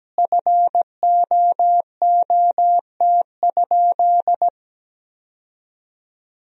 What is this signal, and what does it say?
Morse code, heard as a keyed tone, about 17 wpm, 700 Hz. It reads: FOOT?
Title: 2x Word-Spacing - Sentences from Top 500 Words - 17wpm